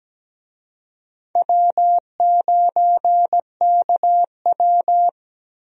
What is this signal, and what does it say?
Morse code, heard as a keyed tone, about 17 wpm, 700 Hz. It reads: W9KW
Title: USA Call Signs - 4 Characters - Repeated 3x - 17wpm